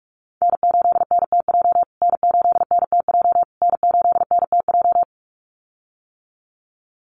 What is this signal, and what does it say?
Morse code, heard as a keyed tone, about 45 wpm, 700 Hz. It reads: N8NTJ N8NTJ N8NTJ
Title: USA Call Signs Repeated 3x 45wpm